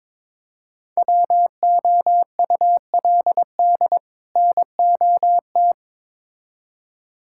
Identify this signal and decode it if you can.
Morse code, heard as a keyed tone, about 21 wpm, 700 Hz. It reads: WOULD NOT
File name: Build A Phrase - 2-Word Phrases - 22wpm